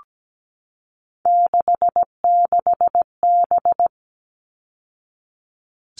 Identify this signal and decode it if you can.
Morse code, heard as a keyed tone, about 17 wpm, 700 Hz. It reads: 66B
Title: All Cut Numbers - 3-Digit 17wpm